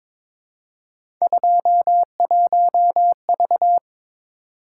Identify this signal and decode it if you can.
Morse code, heard as a keyed tone, about 22 wpm, 700 Hz. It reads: 21V
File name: practice / 22wpm / Three Letter-Number Combinations - 22wpm